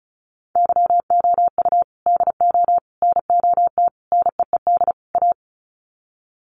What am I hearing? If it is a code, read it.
Morse code, heard as a keyed tone, about 34 wpm, 700 Hz. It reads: YOU DO NOT NEED A
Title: Sets of 5 Words Encore 35wpm